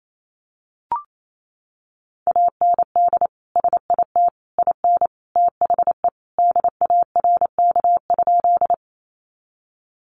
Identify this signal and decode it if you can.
Morse code, heard as a keyed tone, about 28 wpm, 700 Hz. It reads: AND SIT IN THE DARK?
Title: Sentences from Top 500 Words 28wpm